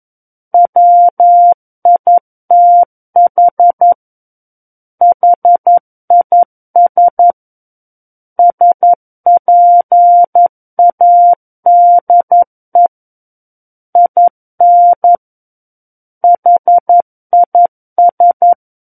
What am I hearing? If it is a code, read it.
Morse code, heard as a keyed tone, about 11 wpm, 700 Hz. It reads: WITH HIS SPADE IN HIS
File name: Book One / 02-Chapter 2 - 11 WPM